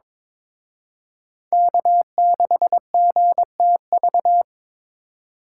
Morse code, heard as a keyed tone, about 26 wpm, 700 Hz. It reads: K6GTV
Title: Speed Racing - USA 1x3 Call Signs 22wpm